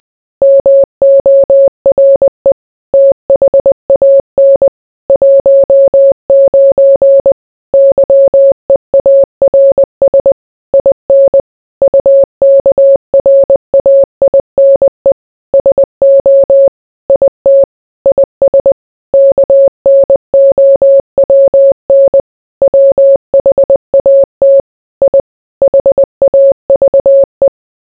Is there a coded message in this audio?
Yes: Morse code from 0.4 to 27.5 s, heard as a keyed tone, about 20 wpm, 550 Hz. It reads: MORE THAN 19 YEARS IN UKRAINE SO IT IS KNOWN WHAT I HAVE